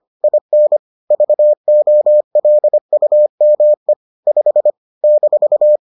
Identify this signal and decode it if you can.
Morse code, heard as a keyed tone, about 25 wpm, 600 Hz. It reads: IN VOLUME 5 -